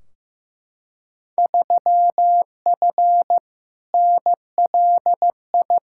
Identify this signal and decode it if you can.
Morse code, heard as a keyed tone, about 15 wpm, 700 Hz. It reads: 3F NLI
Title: Field Day Exchanges Encore - 15wpm